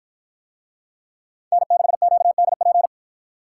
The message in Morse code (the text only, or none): N6QDP